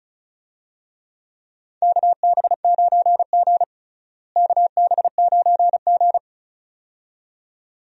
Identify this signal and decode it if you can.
Morse code, heard as a keyed tone, about 36 wpm, 700 Hz. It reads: KB9G KB9G
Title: USA Call Signs - 4 Characters - Repeated 2x - 35wpm